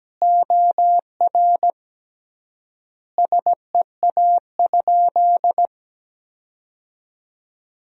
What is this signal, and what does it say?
Morse code, heard as a keyed tone, about 17 wpm, 700 Hz. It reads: OR SEA?
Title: 3x Word-Spacing - Sentences from Top 500 Words - 17wpm